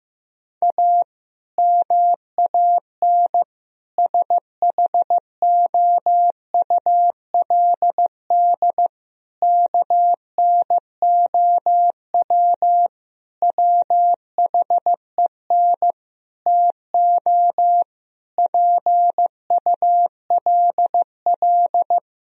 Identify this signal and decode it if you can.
Morse code, heard as a keyed tone, about 15 wpm, 700 Hz. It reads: A MAN SHOULD KNOW WHEN TO PULL
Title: Sentences from Top 1000 Words - Review Part 2 - 15wpm